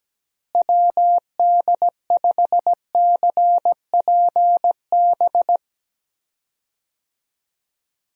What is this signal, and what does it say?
Morse code, heard as a keyed tone, about 17 wpm, 700 Hz. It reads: WD5CPB
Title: USA Call Signs - 6 Characters - Repeated 2x - 17wpm